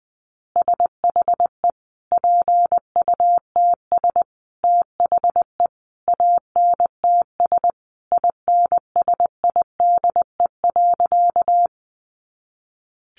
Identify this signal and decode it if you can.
Morse code, heard as a keyed tone, about 20 wpm, 700 Hz. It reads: SHE PUTS THE ANTS INSIDE.
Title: Additional Easy Phrases 20wpm